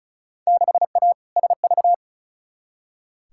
Morse code, heard as a keyed tone, about 34 wpm, 700 Hz. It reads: 6A SV
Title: Field Day Exchanges Encore - 35wpm